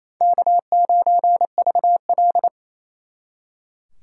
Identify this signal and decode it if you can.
Morse code, heard as a keyed tone, about 36 wpm, 700 Hz. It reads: K9VL